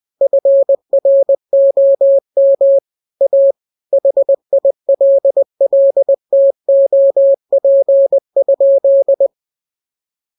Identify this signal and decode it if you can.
Morse code, heard as a keyed tone, about 20 wpm, 550 Hz. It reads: FROM A HILLTOP?
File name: jokes / joke 5 20 0000